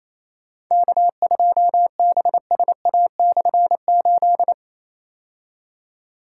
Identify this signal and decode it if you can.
Morse code, heard as a keyed tone, about 28 wpm, 700 Hz. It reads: K2BSA/8
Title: K1USN SST QSO Practice - 28wpm